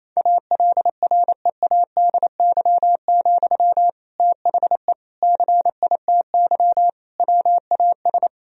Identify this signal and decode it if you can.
Morse code, heard as a keyed tone, about 28 wpm, 700 Hz. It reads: ALREADY, THE CITY WAS
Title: Sentences from Top 1000 Words - Review Part 4 - 28wpm